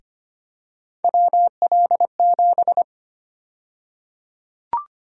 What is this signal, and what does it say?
Morse code, heard as a keyed tone, about 25 wpm, 700 Hz. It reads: WL7
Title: USA Call Sign Prefixes - Repeated 2x - 25wpm